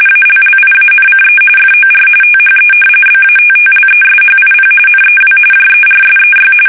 RTTY, about 45.45 baud. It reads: ALLIANCE FOR THE BATTLESHIP NEW JERSEY
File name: fsk850 / battleshipnj